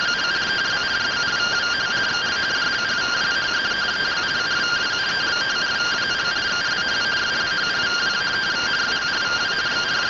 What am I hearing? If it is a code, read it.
Two-tone teleprinter signal, about 170 Hz shift.